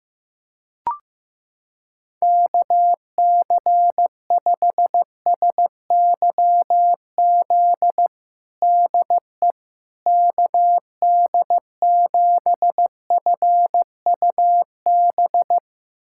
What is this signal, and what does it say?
Morse code, heard as a keyed tone, about 16 wpm, 700 Hz. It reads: KC5SYZ DE KD7FUB
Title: USA Callsign DE Callsign - 6 Characters - 15wpm